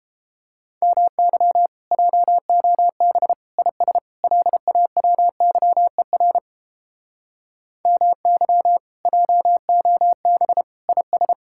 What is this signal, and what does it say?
Morse code, heard as a keyed tone, about 29 wpm, 700 Hz. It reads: MY JOB IS LAWYER MY JOB IS